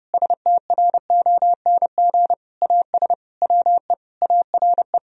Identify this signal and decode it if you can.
Morse code, heard as a keyed tone, about 30 wpm, 700 Hz. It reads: STRONG AS WE ARE
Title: Sentences from Top 1000 Words - Review Part 4 - 30wpm